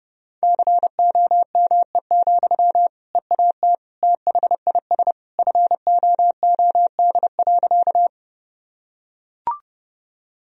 Morse code, heard as a keyed tone, about 30 wpm, 700 Hz. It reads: COME, EAT THIS FOOD.
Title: Sentences from Top 1000 Words - Review Part 1 - 30wpm